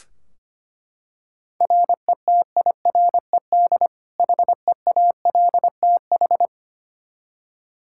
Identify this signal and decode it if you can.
Morse code, heard as a keyed tone, about 25 wpm, 700 Hz. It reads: RETIRED HEALTH
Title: Practice QSO - Exchange 3 - More Info - 25wpm